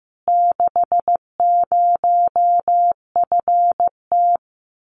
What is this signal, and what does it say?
Morse code, heard as a keyed tone, about 15 wpm, 700 Hz. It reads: 60FT
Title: Common QSO Phrases Encore 15wpm